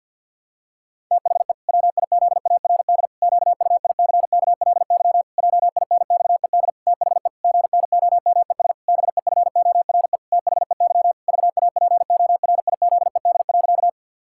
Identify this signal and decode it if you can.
Morse code, heard as a keyed tone, about 48 wpm, 700 Hz. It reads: THE WIZARD QUICKLY JINXED THE GNOMES BEFORE THEY VAPORIZED.
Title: Pangram Challenge v2 - 50wpm